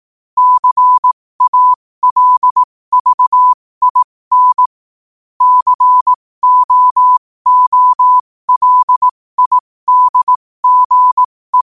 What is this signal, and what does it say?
Morse code, heard as a keyed tone, about 18 wpm, spaced as for 15 wpm, 1000 Hz. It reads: CALVIN COOLIDGE